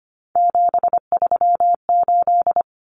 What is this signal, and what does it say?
Morse code, heard as a keyed tone, about 25 wpm, 700 Hz. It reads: 738
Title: Common Cut Numbers - 3-Digit 25wpm